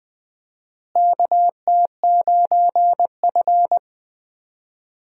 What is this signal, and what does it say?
Morse code, heard as a keyed tone, about 20 wpm, 700 Hz. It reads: KT9F
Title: USA Call Signs - 4 Characters - Repeated 2x - 20wpm